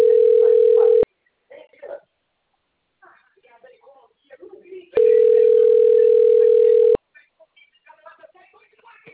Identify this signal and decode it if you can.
Ringback tone (North America), 440 + 480 Hz, with speech.
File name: rg-1600-unknown- - - .99076